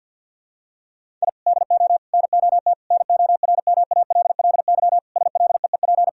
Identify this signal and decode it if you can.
Morse code, heard as a keyed tone, about 52 wpm, 700 Hz. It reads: I DO NOT NORMALLY SLEEP
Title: Sentences from Top 1000 Words - Review Part 6 - 50wpm